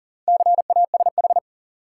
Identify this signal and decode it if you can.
Morse code, heard as a keyed tone, about 40 wpm, 700 Hz. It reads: CASH
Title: Sentences from Top 700 Words - New Word Drill - 40wpm